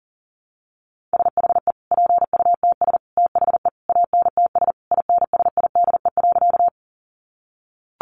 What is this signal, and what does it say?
Morse code, heard as a keyed tone, about 40 wpm, 700 Hz. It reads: SHE PUTS THE ANTS INSIDE.
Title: Easy Phrases 40wpm